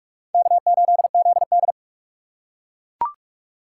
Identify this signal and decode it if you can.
Morse code, heard as a keyed tone, about 42 wpm, 700 Hz. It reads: K8ZD
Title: USA Call Signs - 4 Characters - 2x Character Spacing - 45wpm